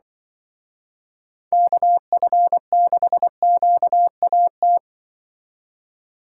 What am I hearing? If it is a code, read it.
Morse code, heard as a keyed tone, about 25 wpm, 700 Hz. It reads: KF6QAT
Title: Speed Racing - USA 2x3 Call Signs 20wpm